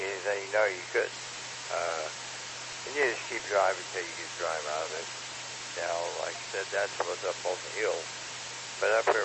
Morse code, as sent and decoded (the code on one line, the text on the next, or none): none